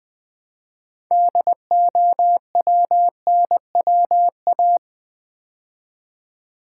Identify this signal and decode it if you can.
Morse code, heard as a keyed tone, about 20 wpm, 700 Hz. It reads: DOWNWA
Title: Build A Word - 3 Letters At A Time - 9-Letter Words - 20wpm